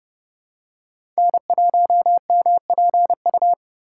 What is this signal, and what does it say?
Morse code, heard as a keyed tone, about 30 wpm, 700 Hz. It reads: N1MPU
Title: USA Call Signs - 5 Characters - 3x Character Spacing - 30wpm